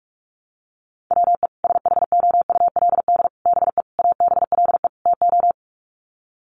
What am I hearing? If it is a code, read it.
Morse code, heard as a keyed tone, about 44 wpm, 700 Hz. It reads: WE SHOULD BE ABLE TO
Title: Sets of 5 Words Encore2 45wpm